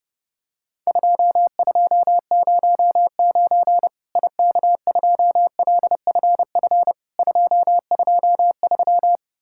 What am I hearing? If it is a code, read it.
Morse code, heard as a keyed tone, about 30 wpm, 700 Hz. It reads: 2209 IK2LFF 223
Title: Worked All Europe DX Contest - QTC Exchange - 30wpm